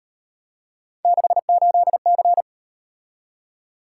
Morse code, heard as a keyed tone, about 37 wpm, 700 Hz. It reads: 68C